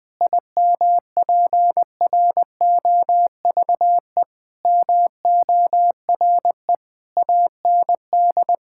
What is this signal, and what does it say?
Morse code, heard as a keyed tone, about 20 wpm, 700 Hz. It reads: IMPROVE MORE AND
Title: Sentences from Top 1000 Words - Review Part 5 - 20wpm